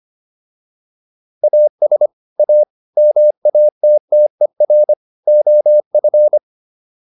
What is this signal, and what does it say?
Morse code, heard as a keyed tone, about 25 wpm, 600 Hz. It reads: AS A MATTER OF
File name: WD304 25